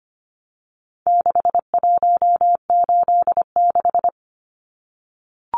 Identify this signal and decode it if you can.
Morse code, heard as a keyed tone, about 25 wpm, 700 Hz. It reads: 6186
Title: Numbers - 4-Digit 25wpm